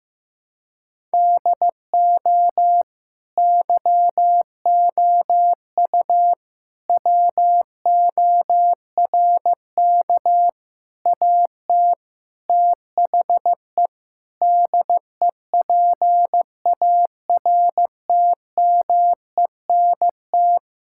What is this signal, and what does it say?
Morse code, heard as a keyed tone, about 15 wpm, 700 Hz. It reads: DO YOU WORK AT THE DEPARTMENT
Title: Sentences from Top 1000 Words - Review Part 5 - 15wpm